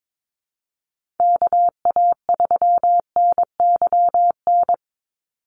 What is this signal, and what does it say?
Morse code, heard as a keyed tone, about 22 wpm, 700 Hz. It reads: KA3NYN
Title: USA Call Signs 22wpm